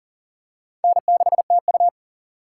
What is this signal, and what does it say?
Morse code, heard as a keyed tone, about 40 wpm, 700 Hz. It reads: N6TU
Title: USA Call Signs - 3x Character Spacing - 40wpm